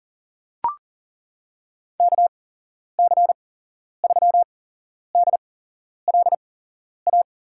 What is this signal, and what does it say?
Morse code, heard as a keyed tone, about 37 wpm, spaced as for 11 wpm, 700 Hz. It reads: KC3DLA
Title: USA Call Signs - 8x Character Spacing - 40wpm